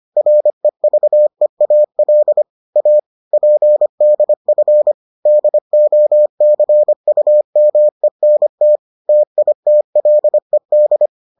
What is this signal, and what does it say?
Morse code, heard as a keyed tone, about 25 wpm, 600 Hz. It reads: REVEAL A PDF DOCUMENT TITLED